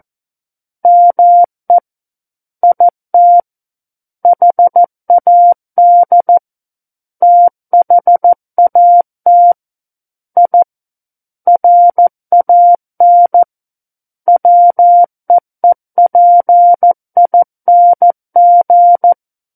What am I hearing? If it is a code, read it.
Morse code, heard as a keyed tone, about 14 wpm, 700 Hz. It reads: ME IT HAD THAT I RAN WEEPING
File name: Book One / 05-Chapter 5 - 14 WPM